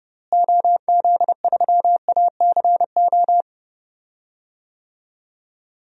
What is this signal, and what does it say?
Morse code, heard as a keyed tone, about 30 wpm, 700 Hz. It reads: OZ3ACO
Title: DXCC-100 Call Signs - 6 Characters - Repeated 3x - 30wpm